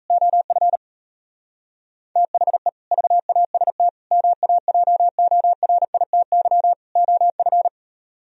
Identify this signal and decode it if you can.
Morse code, heard as a keyed tone, about 33 wpm, 700 Hz. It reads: OF THE VAST MAJORITY OF
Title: Speed-Racing - Sets of 4 Words 28wpm